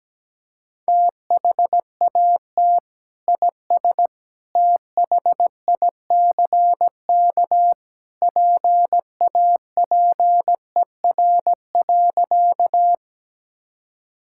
Morse code, heard as a keyed tone, about 17 wpm, 700 Hz. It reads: THAT IS THICK PAPER.
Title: Sentences from Top 1600 Words 17wpm